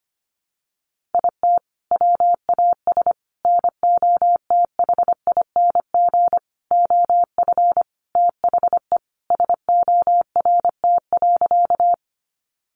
Morse code, heard as a keyed tone, about 25 wpm, 700 Hz. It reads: IT WAS NOTHING OF THE SORT.
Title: Sentences from Top 1300 Words 25wpm